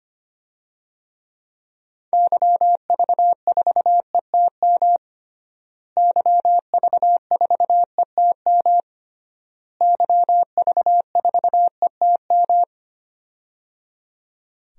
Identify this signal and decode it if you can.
Morse code, heard as a keyed tone, about 25 wpm, 700 Hz. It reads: YV4ETM YV4ETM YV4ETM
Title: DXCC-100 Call Signs - 6 Characters - Repeated 3x - 25wpm